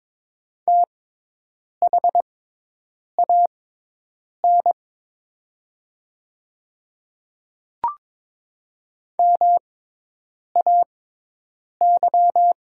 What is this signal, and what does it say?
Morse code, heard as a keyed tone, about 22 wpm, spaced as for 8 wpm, 700 Hz. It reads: THAN MAY
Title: 6x Character Spacing - Top 200 Words - 22wpm